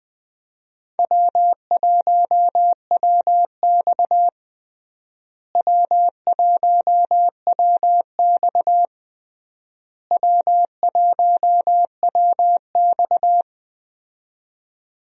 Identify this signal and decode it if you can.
Morse code, heard as a keyed tone, about 20 wpm, 700 Hz. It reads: W1WX W1WX W1WX
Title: USA Call Signs - 4 Characters - Repeated 3x - 20wpm